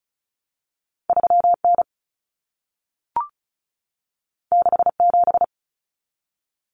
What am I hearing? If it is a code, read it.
Morse code, heard as a keyed tone, about 36 wpm, 700 Hz. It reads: 3N 67
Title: Common Cut Numbers - 2-Digit 35wpm